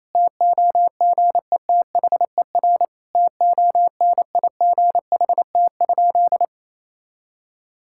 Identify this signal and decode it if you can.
Morse code, heard as a keyed tone, about 28 wpm, 700 Hz. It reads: TOGETHER TONIGHT?
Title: Sentences from Top 1600 Words 28wpm